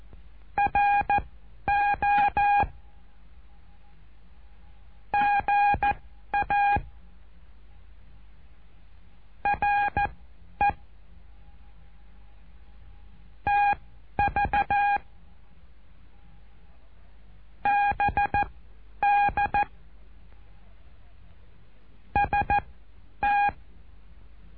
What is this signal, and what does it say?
Morse code, heard as a keyed tone, about 14 wpm, spaced as for 10 wpm, 820 Hz. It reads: RO GA RE TV BD ST
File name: cursos / CURSO DE TELEGRAFIA / Telegrafia - Trilha 08